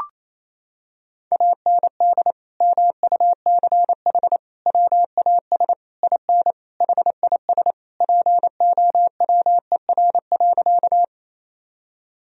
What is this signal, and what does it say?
Morse code, heard as a keyed tone, about 27 wpm, 700 Hz. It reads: AND MUCH WAS IN HIS POWER.
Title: Sentences from Top 1000 Words - Review Part 2 - 28wpm